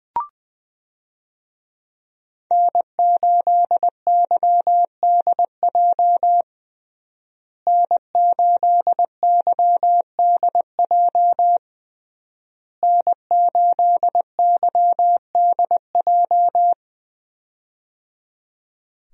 Morse code, heard as a keyed tone, about 20 wpm, 700 Hz. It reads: N8YDJ N8YDJ N8YDJ
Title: USA Call Signs - 5 Characters - Repeated 3x - 20wpm